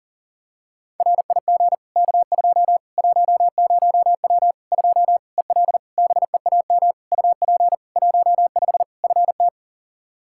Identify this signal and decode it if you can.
Morse code, heard as a keyed tone, about 40 wpm, 700 Hz. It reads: RIG K2 10W 2 EL BEAM UP 15 FT
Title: Practice QSO - Exchange 2 - Info - 40wpm